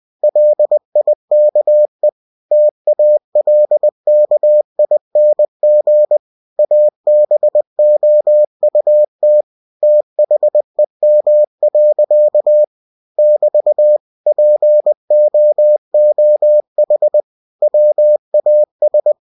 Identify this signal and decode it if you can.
Morse code, heard as a keyed tone, about 20 wpm, 600 Hz. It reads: LIKE TALKING ABOUT THEM. <BT> POOH WAS
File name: cw-books / winnie-the-pooh / winnie-20wpm-page-0093